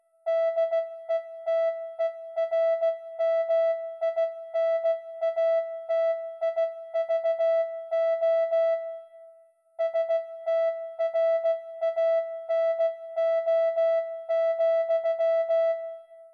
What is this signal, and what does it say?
Morse code, heard as a keyed tone, about 16 wpm, 660 Hz. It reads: DETERMINATIVO STRANO,